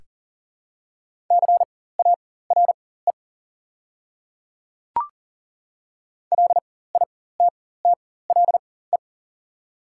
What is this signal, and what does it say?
Morse code, heard as a keyed tone, about 40 wpm, spaced as for 19 wpm, 700 Hz. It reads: CARE LITTLE
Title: 4x Character Spacing - Top 300 Words - 40wpm